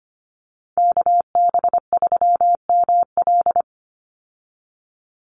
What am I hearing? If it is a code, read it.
Morse code, heard as a keyed tone, about 25 wpm, 700 Hz. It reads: KB3ML
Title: Call Signs - 25wpm